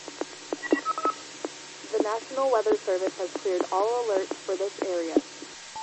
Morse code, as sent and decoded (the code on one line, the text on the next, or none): none